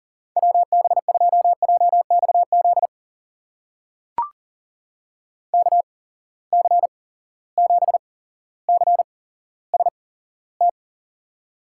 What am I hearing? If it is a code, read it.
Morse code, heard as a keyed tone, about 40 wpm, spaced as for 17 wpm, 700 Hz. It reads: WB2JXZ KC7CST
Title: USA Call Signs - 8x Character Spacing - 40wpm